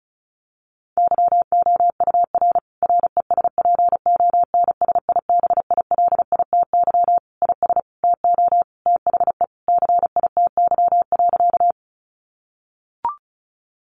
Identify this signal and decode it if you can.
Morse code, heard as a keyed tone, about 34 wpm, 700 Hz. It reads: YOUR RESPONSIBILITY IS TO THE CITY.
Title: Sentences from Top 1200 Words 35wpm